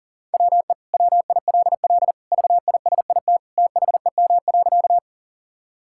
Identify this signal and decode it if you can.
Morse code, heard as a keyed tone, about 40 wpm, 700 Hz. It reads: WE WILL VISIT THEM.
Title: Sentences from Top 1000 Words - Review Part 6 - 40wpm